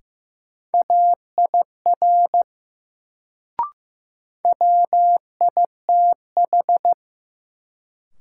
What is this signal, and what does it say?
Morse code, heard as a keyed tone, about 15 wpm, 700 Hz. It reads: AIR WITH
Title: Sets of 2 Words - New Word Drill - 15wpm